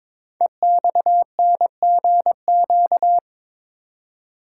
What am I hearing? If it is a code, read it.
Morse code, heard as a keyed tone, about 23 wpm, 700 Hz. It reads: EXNGQ
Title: Code Groups - 5 Letters - 22wpm